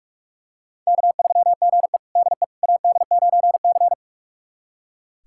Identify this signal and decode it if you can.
Morse code, heard as a keyed tone, about 46 wpm, 700 Hz. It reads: K3GE DE AD9C
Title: USA Callsign DE Callsign - 4 Characters - 45wpm